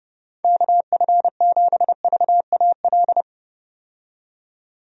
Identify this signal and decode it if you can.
Morse code, heard as a keyed tone, about 32 wpm, 700 Hz. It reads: KF7VAL
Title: USA Call Signs - 6 Characters - 30wpm